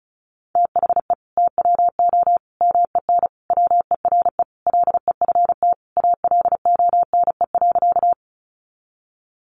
Morse code, heard as a keyed tone, about 34 wpm, 700 Hz. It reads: THE TWO MEN WERE LEFT ALONE.
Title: Sentences from Top 1000 Words 35wpm